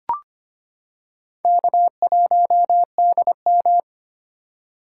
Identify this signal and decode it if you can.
Morse code, heard as a keyed tone, about 25 wpm, 700 Hz. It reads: K1DM
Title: K1USN SST QSO Practice - 25wpm